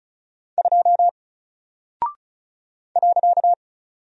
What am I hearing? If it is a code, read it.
Morse code, heard as a keyed tone, about 34 wpm, 700 Hz. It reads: 2 .
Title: Single Character (Letter:Number:Punctuation:Prosign) 35wpm